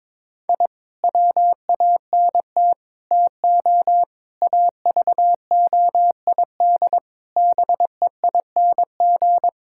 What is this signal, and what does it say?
Morse code, heard as a keyed tone, about 22 wpm, 700 Hz. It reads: I WANT TO AVOID BEING
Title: Sentences from Top 1000 Words - Review Part 6 - 22wpm